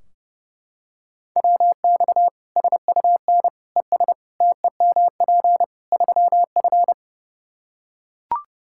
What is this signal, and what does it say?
Morse code, heard as a keyed tone, about 32 wpm, 700 Hz. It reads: WX SUN ES TEMP 3F
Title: K1USN SST QSO Practice 30wpm